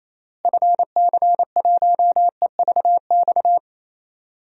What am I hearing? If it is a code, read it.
Morse code, heard as a keyed tone, about 28 wpm, 700 Hz. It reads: FC1EVX
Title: DXCC-100 Call Signs - 6 Characters - Repeated 3x - 28wpm